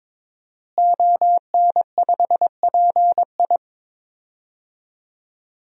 Morse code, heard as a keyed tone, about 23 wpm, 700 Hz. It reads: ON5PI
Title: DXCC-100 Call Signs - 5 Characters - Repeated 3x - 22wpm